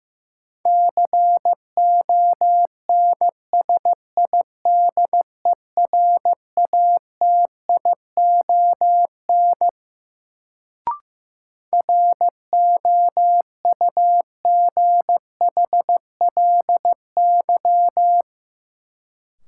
Morse code, heard as a keyed tone, about 15 wpm, 700 Hz. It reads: CONSIDERATION ROUGHLY
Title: Sentences from Top 1500 Words - New Word Drill - 15wpm